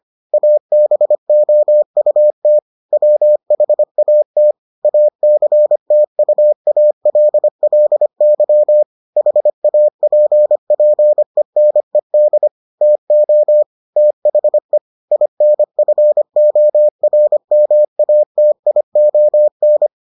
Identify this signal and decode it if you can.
Morse code, heard as a keyed tone, about 25 wpm, 600 Hz. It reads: ABOUT WHAT ACTUALLY HAPPENED TO THE INFORMATION